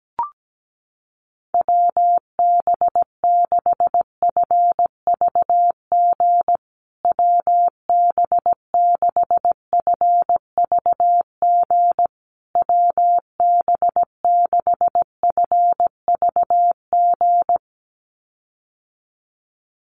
Morse code, heard as a keyed tone, about 17 wpm, 700 Hz. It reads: WB6FVG WB6FVG WB6FVG
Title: USA Call Signs Repeated 3x 17wpm